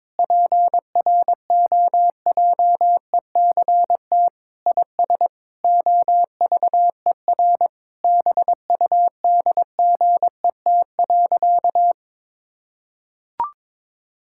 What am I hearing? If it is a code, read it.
Morse code, heard as a keyed tone, about 22 wpm, 700 Hz. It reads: PROJECT IS OVER BUDGET.
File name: practice / 22wpm / Sentences with 8-Word Phrases - 22wpm